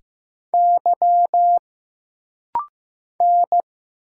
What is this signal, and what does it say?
Morse code, heard as a keyed tone, about 15 wpm, 700 Hz. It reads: Y N
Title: Single Letter-Number Alt - Rapid-Fire 15wpm